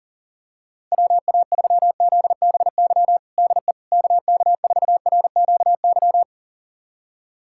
Morse code, heard as a keyed tone, about 40 wpm, 700 Hz. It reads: WA3ZBY DE KK4RQY
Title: USA Callsign DE Callsign 40wpm